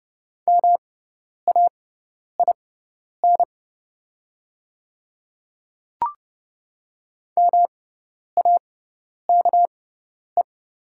Morse code, heard as a keyed tone, about 30 wpm, spaced as for 10 wpm, 690 Hz. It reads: MAIN MAKE
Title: 6x Character Spacing - Top 500 Words - 30wpm